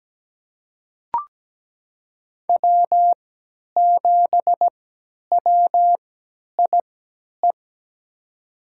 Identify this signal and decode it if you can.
Morse code, heard as a keyed tone, about 17 wpm, spaced as for 10 wpm, 700 Hz. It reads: W7WIE
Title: USA Call Signs - 5 Characters - 3x Character Spacing - 17wpm